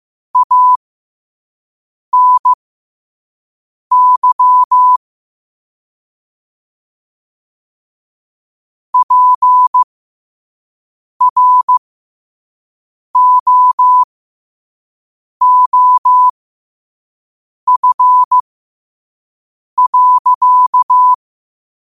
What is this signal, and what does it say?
Morse code, heard as a keyed tone, about 15 wpm, spaced as for 5 wpm, 1000 Hz. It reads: ANY PROOF.
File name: qotdfiles / 5wpm